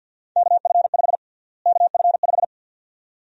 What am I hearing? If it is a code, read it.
Morse code, heard as a keyed tone, about 52 wpm, 700 Hz. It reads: KF5 KF5
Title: USA Call Sign Prefixes - Repeated 2x - 50wpm